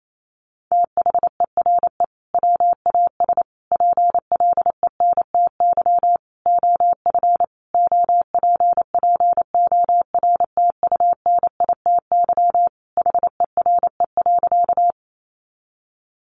Morse code, heard as a keyed tone, about 28 wpm, 700 Hz. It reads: THERE WAS PLENTY OF OPPORTUNITY HERE.
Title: Sentences from Top 1300 Words 28wpm